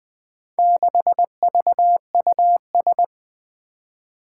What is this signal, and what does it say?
Morse code, heard as a keyed tone, about 21 wpm, 700 Hz. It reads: 6VUS
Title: Four Challenging Letter-Number Combinations 20wpm